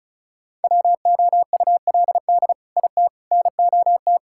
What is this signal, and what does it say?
Morse code, heard as a keyed tone, about 36 wpm, 700 Hz. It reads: WOULD IT NOT
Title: Sentences from Top 1600 Words 35wpm